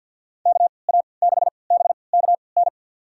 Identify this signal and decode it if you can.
Morse code, heard as a keyed tone, about 46 wpm, spaced as for 27 wpm, 700 Hz. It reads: KA6BKN
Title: USA Call Signs - 6 Characters - 3x Character Spacing - 50wpm